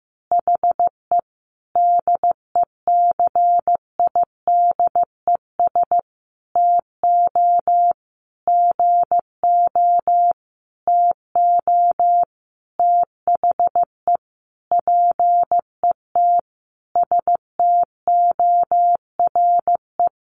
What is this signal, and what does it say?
Morse code, heard as a keyed tone, about 15 wpm, 700 Hz. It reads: HE DECIDES TO GO TO THE PET STORE
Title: Additional Easy Phrases 15wpm